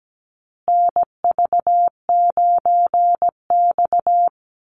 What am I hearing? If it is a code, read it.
Morse code, heard as a keyed tone, about 17 wpm, 700 Hz. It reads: NV9X
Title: Call Signs Repeated 2x - 17wpm